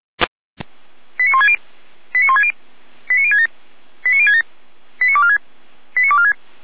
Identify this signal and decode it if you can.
Five-tone selective call (ZVEI), heard as quick runs of five tones, tones of about 70 ms. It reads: 89160 89160 89067 89067 89246 89246
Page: /455/SDR/ZVEI/89078/